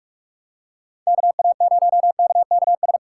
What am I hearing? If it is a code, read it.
Morse code, heard as a keyed tone, about 46 wpm, 700 Hz. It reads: KA0KKS
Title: USA Call Signs - 6 Characters - 2x Character Spacing - 45wpm